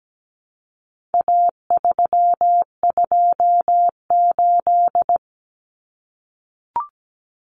Morse code, heard as a keyed tone, about 17 wpm, 700 Hz. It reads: A328
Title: Common Cut Numbers - 4-Digit 17wpm